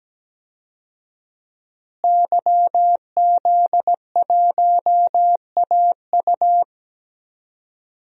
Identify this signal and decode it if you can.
Morse code, heard as a keyed tone, about 17 wpm, 700 Hz. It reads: YZ1AU
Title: DXCC-100 Call Signs - 5 Characters - Repeated 2x - 17wpm